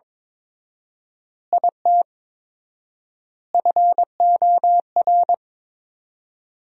Morse code, heard as a keyed tone, about 22 wpm, 700 Hz. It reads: IT FOR
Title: 4x Word-Spacing - Sentences from Top 200 Words - 22wpm